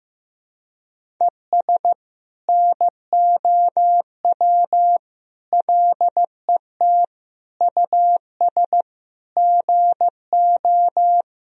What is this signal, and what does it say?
Morse code, heard as a keyed tone, about 15 wpm, 700 Hz. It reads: ES NOW LET US GO
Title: Sentences from Top 300 Words and Abbreviations 15wpm